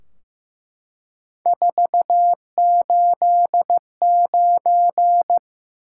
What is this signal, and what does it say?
Morse code, heard as a keyed tone, about 15 wpm, 700 Hz. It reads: 489
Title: Top 180 QSO Elements 15wpm